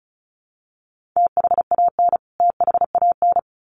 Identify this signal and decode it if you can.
Morse code, heard as a keyed tone, about 34 wpm, 700 Hz. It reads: THAN THAN
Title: Top 500 Words Repeated 35wpm